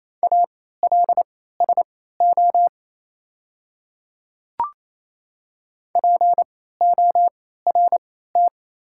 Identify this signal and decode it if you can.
Morse code, heard as a keyed tone, about 28 wpm, spaced as for 16 wpm, 700 Hz. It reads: ALSO PORT
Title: 3x Character Spacing - Top 200 Words - 28wpm